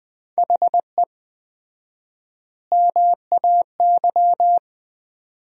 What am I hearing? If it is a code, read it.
Morse code, heard as a keyed tone, about 20 wpm, 700 Hz. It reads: HE MAY
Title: 4x Word-Spacing - Sentences from Top 300 Words - 20wpm